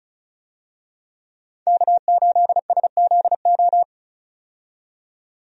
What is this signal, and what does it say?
Morse code, heard as a keyed tone, about 36 wpm, 700 Hz. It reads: K8SZO
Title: USA Call Signs - 5 Characters - 35wpm